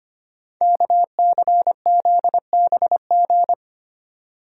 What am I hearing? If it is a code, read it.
Morse code, heard as a keyed tone, about 25 wpm, 700 Hz. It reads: KCZBG
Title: Code Groups - 5 Letters - 25wpm